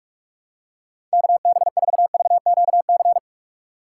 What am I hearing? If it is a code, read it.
Morse code, heard as a keyed tone, about 44 wpm, 700 Hz. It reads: KB4VQC